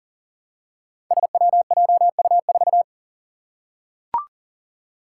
Morse code, heard as a keyed tone, about 39 wpm, 700 Hz. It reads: SWJU4
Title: Five Challenging Letter-Number Combinations 40wpm